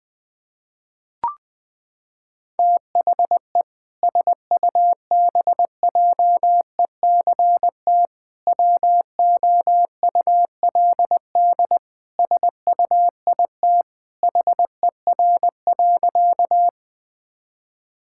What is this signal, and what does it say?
Morse code, heard as a keyed tone, about 20 wpm, 700 Hz. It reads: THE SUBJECT WOULD SUIT HER.
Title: Sentences from Top 1600 Words 20wpm